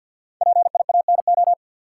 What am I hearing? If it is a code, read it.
Morse code, heard as a keyed tone, about 46 wpm, 700 Hz. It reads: PIANO
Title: Five Letter Words - 50wpm